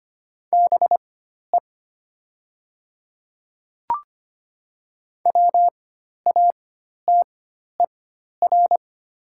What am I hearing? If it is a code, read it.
Morse code, heard as a keyed tone, about 25 wpm, spaced as for 12 wpm, 700 Hz. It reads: BE WATER
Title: 4x Character Spacing - Top 200 Words - 25wpm